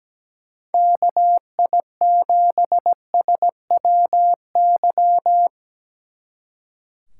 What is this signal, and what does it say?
Morse code, heard as a keyed tone, about 19 wpm, 700 Hz. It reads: KI7SWY
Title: Speed Racing - USA 2x3 Call Signs 17wpm